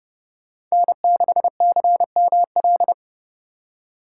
Morse code, heard as a keyed tone, about 30 wpm, 700 Hz. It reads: N6CML
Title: USA Call Signs - 4x Character Spacing - 30wpm